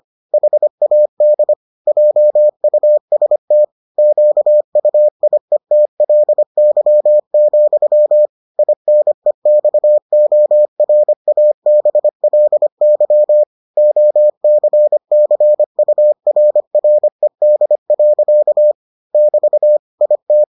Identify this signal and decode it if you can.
Morse code, heard as a keyed tone, about 25 wpm, 600 Hz. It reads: HAD JUST QUIETLY, INEXORABLY OCCURRED. <BT> IT